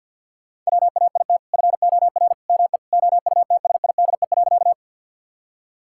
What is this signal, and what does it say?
Morse code, heard as a keyed tone, about 50 wpm, 700 Hz. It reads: WAIT FOR ME OUTSIDE.
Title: Sentences from Top 1000 Words - Review Part 6 - 50wpm